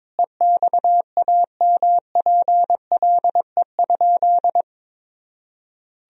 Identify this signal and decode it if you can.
Morse code, heard as a keyed tone, about 22 wpm, 700 Hz. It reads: EXAMPLE?
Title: Sentences from Top 1000 Words - Review Part 1 - 22wpm